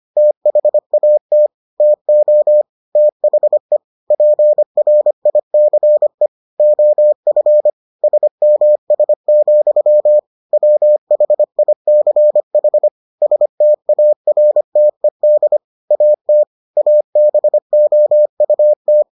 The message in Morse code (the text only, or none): THAT TO THE PRICE OF SMS, WHICH STARTED AT ABOUT